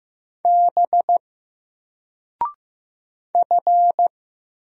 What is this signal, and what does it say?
Morse code, heard as a keyed tone, about 15 wpm, 700 Hz. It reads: B F
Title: Single Character (Letter:Number:Punctuation:Prosign) 15wpm